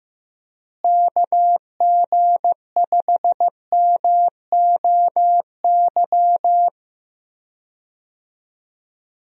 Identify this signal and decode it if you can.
Morse code, heard as a keyed tone, about 15 wpm, 700 Hz. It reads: KG5MOY
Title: USA Call Signs - 6 Characters - Repeated 2x - 15wpm